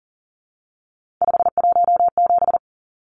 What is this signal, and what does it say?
Morse code, heard as a keyed tone, about 40 wpm, 700 Hz. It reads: 517
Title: Numbers - 3-Digit 40wpm